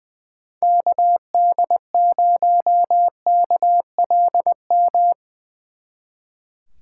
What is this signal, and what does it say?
Morse code, heard as a keyed tone, about 20 wpm, 700 Hz. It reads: KD0KLM
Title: Speed Racing - USA 2x3 Call Signs 20wpm